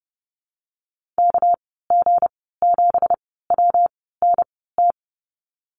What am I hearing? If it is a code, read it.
Morse code, heard as a keyed tone, about 30 wpm, spaced as for 17 wpm, 700 Hz. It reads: KG7WNT
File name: Lesson 37g - Letter x - Call Signs 30wpm